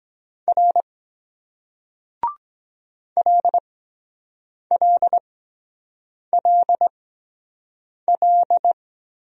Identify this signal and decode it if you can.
Morse code, heard as a keyed tone, about 23 wpm, spaced as for 7 wpm, 700 Hz. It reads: R LLLL